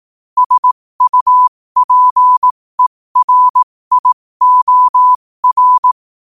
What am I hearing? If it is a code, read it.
Morse code, heard as a keyed tone, about 18 wpm, spaced as for 15 wpm, 1000 Hz. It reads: SUPERIOR
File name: qotdfiles / 15wpm